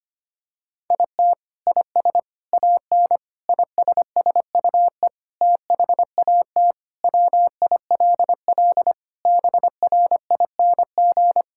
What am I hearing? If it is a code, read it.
Morse code, heard as a keyed tone, about 25 wpm, 700 Hz. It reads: IT IS AN ISSUE THAT WILL BRING
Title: Sentences from Top 1000 Words - Review Part 6 - 25wpm